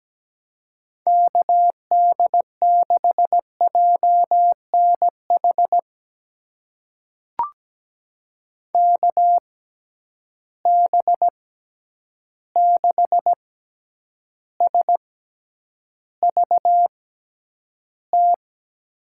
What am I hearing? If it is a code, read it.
Morse code, heard as a keyed tone, about 17 wpm, spaced as for 9 wpm, 700 Hz. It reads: KD6JNH KB6SVT
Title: USA Call Signs - 6 Characters - 6x Character Spacing - 17wpm